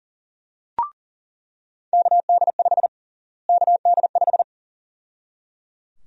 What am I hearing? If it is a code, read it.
Morse code, heard as a keyed tone, about 37 wpm, 700 Hz. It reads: KD5 KD5
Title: USA Call Sign Prefixes - Repeated 2x - 40wpm